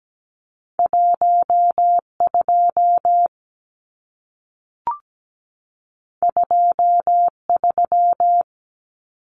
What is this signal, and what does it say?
Morse code, heard as a keyed tone, about 17 wpm, 700 Hz. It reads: 12 23
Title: Numbers - 2-Digit 17wpm